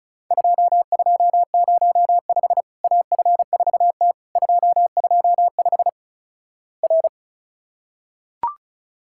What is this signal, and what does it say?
Morse code, heard as a keyed tone, about 36 wpm, 700 Hz. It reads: 2205 AF4T 225 R
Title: Worked All Europe DX Contest - QTC Exchange - 35wpm